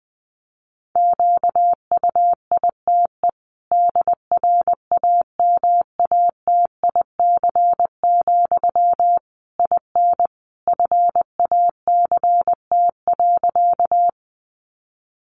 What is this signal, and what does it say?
Morse code, heard as a keyed tone, about 20 wpm, 700 Hz. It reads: QUITE DRAMATIC, IN FACT.
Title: Sentences from Top 1300 Words 20wpm